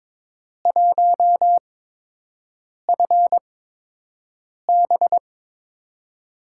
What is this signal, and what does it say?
Morse code, heard as a keyed tone, about 23 wpm, spaced as for 6 wpm, 700 Hz. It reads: 1FB
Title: USA Call Signs - 8x Character Spacing - 22wpm